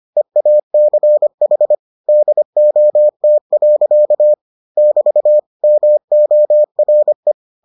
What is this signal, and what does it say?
Morse code, heard as a keyed tone, about 25 wpm, 600 Hz. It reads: EACH DOT. <BT> MORE